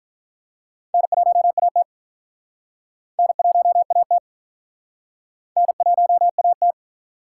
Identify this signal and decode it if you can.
Morse code, heard as a keyed tone, about 47 wpm, 700 Hz. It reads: N1AT N1AT N1AT